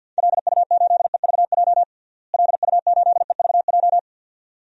Morse code, heard as a keyed tone, about 52 wpm, 700 Hz. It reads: LU8EVJ LU8EVJ
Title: DXCC-100 Call Signs - 6 Characters - Repeated 2x - 50wpm